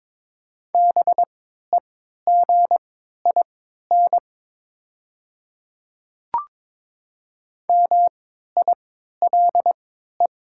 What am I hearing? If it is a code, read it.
Morse code, heard as a keyed tone, about 22 wpm, spaced as for 13 wpm, 700 Hz. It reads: BEGIN MILE
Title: 3x Character Spacing - Top 500 Words - 22wpm